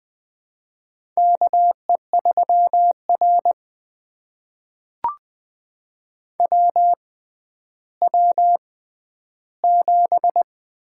Morse code, heard as a keyed tone, about 20 wpm, spaced as for 11 wpm, 700 Hz. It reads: KE3R WW7
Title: USA Call Signs - 4 Characters - 6x Character Spacing - 20wpm